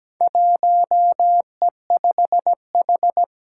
Morse code, heard as a keyed tone, about 17 wpm, 700 Hz. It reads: 1E5H
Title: Four Challenging Letter-Number Combinations 17wpm